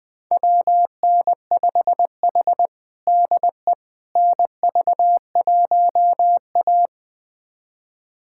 Morse code, heard as a keyed tone, about 20 wpm, 700 Hz. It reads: WN5H DE NV1A
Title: USA Callsign DE Callsign - 4 Characters - 20wpm